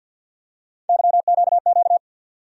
Morse code, heard as a keyed tone, about 52 wpm, 700 Hz. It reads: YQQ
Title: USA Call Sign Postfixes - 3 Letters 50wpm